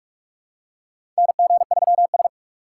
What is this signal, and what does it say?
Morse code, heard as a keyed tone, about 46 wpm, 700 Hz. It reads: NG3S
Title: USA Call Signs - 4 Characters - 3x Character Spacing - 45wpm